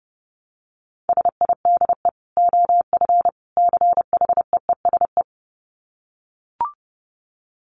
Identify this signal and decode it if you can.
Morse code, heard as a keyed tone, about 30 wpm, 700 Hz. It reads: SIDE OF CHEESE
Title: Dit Challenge 30wpm